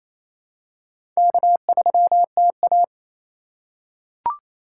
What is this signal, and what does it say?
Morse code, heard as a keyed tone, about 28 wpm, 700 Hz. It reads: K3TA
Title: USA Call Signs - 4 Characters - 2x Character Spacing - 28wpm